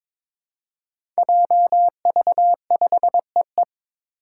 Morse code, heard as a keyed tone, about 21 wpm, 700 Hz. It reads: JV5EE
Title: Five Challenging Letter-Number Combinations 22wpm